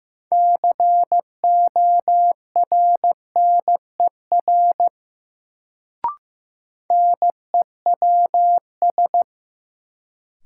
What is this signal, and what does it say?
Morse code, heard as a keyed tone, about 15 wpm, 700 Hz. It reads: CORNER NEWS
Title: Sets of 5 Words - New Word Drill - 15wpm